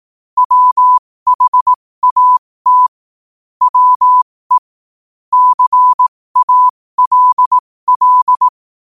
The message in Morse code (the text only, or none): WHAT WE CALL